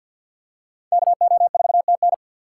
Morse code, heard as a keyed tone, about 51 wpm, 700 Hz. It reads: KO4TN
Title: USA Call Signs - 5 Characters - 4x Character Spacing - 50wpm